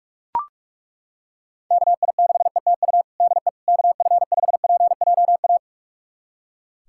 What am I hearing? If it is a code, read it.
Morse code, heard as a keyed tone, about 41 wpm, 700 Hz. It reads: KI6ETU DE KF5PJA
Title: USA Callsign DE Callsign - 6 Characters - 45wpm